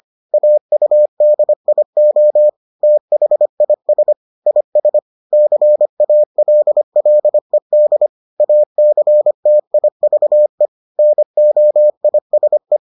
Morse code, heard as a keyed tone, about 25 wpm, 600 Hz. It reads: AUDIO THIS IS CALLED ACTIVE NOISE